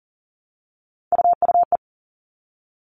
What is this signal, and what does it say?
Morse code, heard as a keyed tone, about 40 wpm, 700 Hz. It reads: UUE
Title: All Cut Numbers - 3-Digit 40wpm